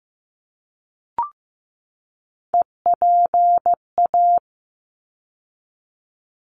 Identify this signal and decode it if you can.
Morse code, heard as a keyed tone, about 15 wpm, 700 Hz. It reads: EPA
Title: ARRL Sections 15wpm